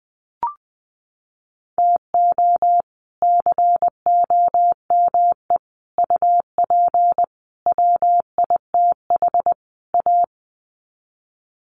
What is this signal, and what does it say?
Morse code, heard as a keyed tone, about 20 wpm, 700 Hz. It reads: TO COME UP WITH A
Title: Sets of 5 Words 20wpm